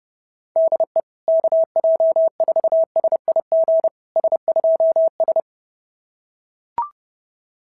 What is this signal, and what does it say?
Morse code, heard as a keyed tone, about 28 wpm, 650 Hz. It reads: DE KJ4SIG S2S
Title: SOTA QSO Practice 30wpm